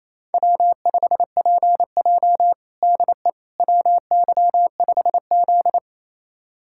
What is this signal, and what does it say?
Morse code, heard as a keyed tone, about 28 wpm, 700 Hz. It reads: W5PJ DE WY5Z
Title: USA Callsign DE Callsign - 4 Characters - 28wpm